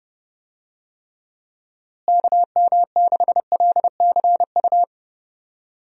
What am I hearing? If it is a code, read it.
Morse code, heard as a keyed tone, about 32 wpm, 700 Hz. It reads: KM6LCU
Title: USA Call Signs - 6 Characters - Repeated 3x - 30wpm